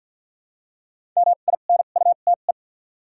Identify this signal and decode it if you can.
Morse code, heard as a keyed tone, about 48 wpm, spaced as for 35 wpm, 700 Hz. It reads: MINUTE
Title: 2x Character Spacing - Top 500 Words - 50wpm